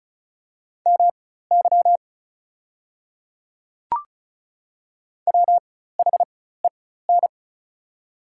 Morse code, heard as a keyed tone, about 34 wpm, spaced as for 16 wpm, 690 Hz. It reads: MY WHEN
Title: 4x Character Spacing - Top 200 Words - 35wpm